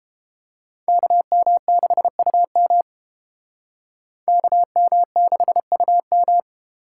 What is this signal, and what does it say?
Morse code, heard as a keyed tone, about 29 wpm, 700 Hz. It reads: KM6UM KM6UM